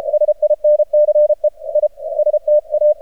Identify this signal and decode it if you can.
Morse code, heard as a keyed tone, about 30 wpm, 600 Hz. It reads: SINCE I STA
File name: archived / CRSnet